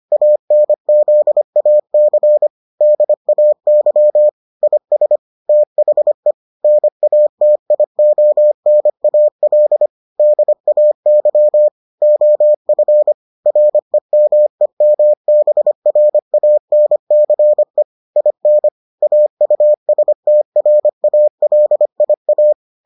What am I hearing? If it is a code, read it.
Morse code, heard as a keyed tone, about 25 wpm, 600 Hz. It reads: ANZAC DAY IS THE NATIONAL DAY OF REMEMBRANCE IN AUSTRALIA